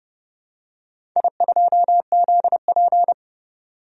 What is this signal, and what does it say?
Morse code, heard as a keyed tone, about 30 wpm, 700 Hz. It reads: I2ZP
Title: DXCC-100 Call Signs - 4 Characters - 30wpm